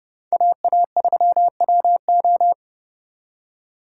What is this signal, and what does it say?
Morse code, heard as a keyed tone, about 37 wpm, 700 Hz. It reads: AA3WO
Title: Speed Racing - USA 2x2 Call Signs 30wpm